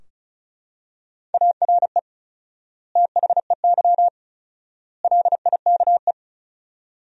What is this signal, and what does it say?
Morse code, heard as a keyed tone, about 34 wpm, 700 Hz. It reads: ARE THEY LIKE
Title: 4x Word-Spacing - Sentences from Top 200 Words - 35wpm